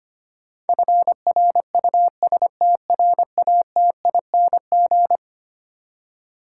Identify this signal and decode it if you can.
Morse code, heard as a keyed tone, about 25 wpm, 700 Hz. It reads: FRUSTRATING
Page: morse-code-ninja-practice-25wpm